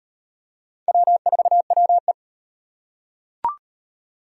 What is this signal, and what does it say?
Morse code, heard as a keyed tone, about 38 wpm, 700 Hz. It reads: W4WE